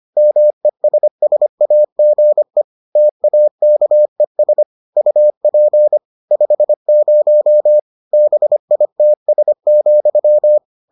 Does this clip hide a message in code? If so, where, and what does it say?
Yes: Morse code from 0.2 to 10.6 s, heard as a keyed tone, about 25 wpm, 600 Hz. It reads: MESSAGE TAKES UP 50 BITS,